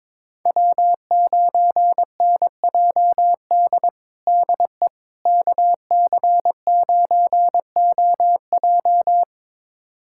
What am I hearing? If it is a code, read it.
Morse code, heard as a keyed tone, about 22 wpm, 700 Hz. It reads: W9NJD DE KC9OJ
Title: USA Callsign DE Callsign - 5 Characters - 22wpm